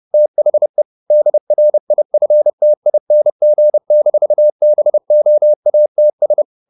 Morse code, heard as a keyed tone, about 30 wpm, 600 Hz. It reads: THE DRIFTING-BOATS